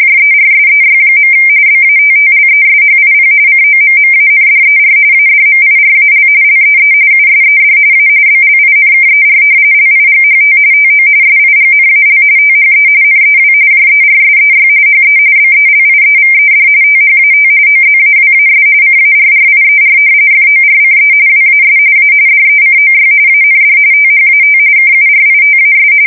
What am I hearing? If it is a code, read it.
RTTY, about 45.45 baud. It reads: JERSEY. BATTLESHIP COMMANDER ADMIRAL LEE USED A BLINKER LIGHT TO SIGNAL THE NEW JERSEY HE BELIEVED THAT OZAWAS CARRIER FORCE WAS A DECOY AND THAT KURITAS
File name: fsk170 / battleshipnj